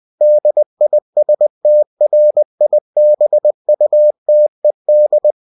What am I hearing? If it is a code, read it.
Morse code, heard as a keyed tone, about 20 wpm, 600 Hz. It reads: DISTRIBUTED
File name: cw-books / winnie-the-pooh / winnie-20wpm-page-0122